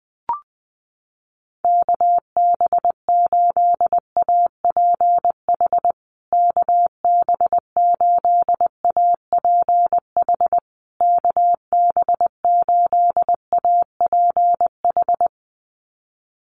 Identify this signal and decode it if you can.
Morse code, heard as a keyed tone, about 20 wpm, 700 Hz. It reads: KB8APH KB8APH KB8APH
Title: USA Call Signs Repeated 3x 20wpm